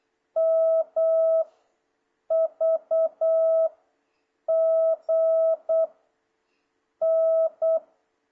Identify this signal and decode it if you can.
Morse code, heard as a keyed tone, about 8 wpm, spaced as for 6 wpm, 650 Hz. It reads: MVG N
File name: jh8cbh / 2026cw